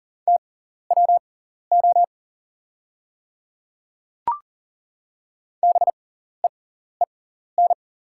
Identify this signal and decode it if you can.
Morse code, heard as a keyed tone, about 40 wpm, spaced as for 14 wpm, 700 Hz. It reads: TWO BEEN
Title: 6x Character Spacing - Top 300 Words - 40wpm